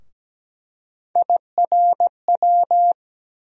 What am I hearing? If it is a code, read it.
Morse code, heard as a keyed tone, about 17 wpm, 700 Hz. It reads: IRW
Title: USA Call Sign Postfixes - 3 Letters 17wpm